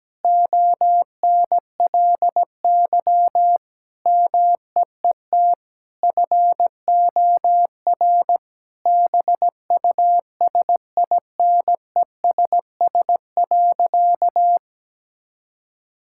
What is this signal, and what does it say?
Morse code, heard as a keyed tone, about 17 wpm, 700 Hz. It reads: ONLY MEET FOR BUSINESS.
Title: Sentences from Top 1000 Words - Review Part 6 - 17wpm